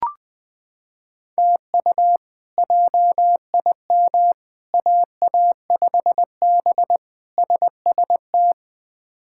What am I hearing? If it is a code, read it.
Morse code, heard as a keyed tone, about 21 wpm, 700 Hz. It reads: TU JIM AA5B SST
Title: K1USN SST QSO Practice - 20wpm